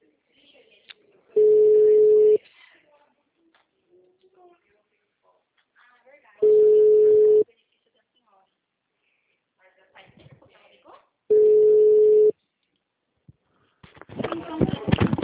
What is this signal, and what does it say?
Ringback tone (Europe, 425 Hz), with speech.